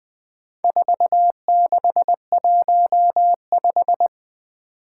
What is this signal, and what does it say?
Morse code, heard as a keyed tone, about 20 wpm, 700 Hz. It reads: 4615
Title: Four Challenging Letter-Number Combinations 20wpm